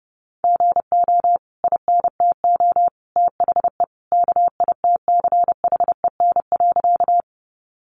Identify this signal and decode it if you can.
Morse code, heard as a keyed tone, about 30 wpm, 700 Hz. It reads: GO INTO THE KITCHEN.
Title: Sentences from Top 1400 Words 30wpm